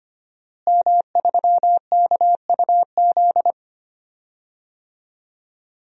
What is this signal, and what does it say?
Morse code, heard as a keyed tone, about 25 wpm, 700 Hz. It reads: M3KUZ
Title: DXCC-100 Call Signs - 5 Characters - Repeated 2x - 25wpm